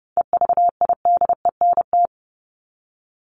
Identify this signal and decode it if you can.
Morse code, heard as a keyed tone, about 30 wpm, 700 Hz. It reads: EVIDENT
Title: LY Suffix - Top 100 Words with Base Words 30wpm